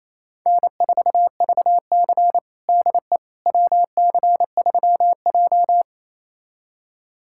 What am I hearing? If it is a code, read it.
Morse code, heard as a keyed tone, about 28 wpm, 700 Hz. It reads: N4VC DE WC3J
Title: USA Callsign DE Callsign - 4 Characters - 28wpm